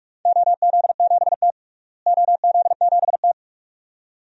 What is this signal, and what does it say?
Morse code, heard as a keyed tone, about 46 wpm, 700 Hz. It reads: OZ7T OZ7T
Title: DXCC-100 Call Signs - 4 Characters - Repeated 2x - 45wpm